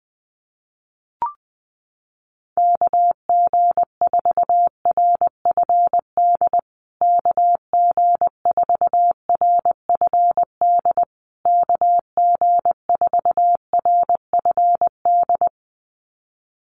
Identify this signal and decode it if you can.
Morse code, heard as a keyed tone, about 20 wpm, 700 Hz. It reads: KG4RFD KG4RFD KG4RFD
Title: USA Call Signs Repeated 3x 20wpm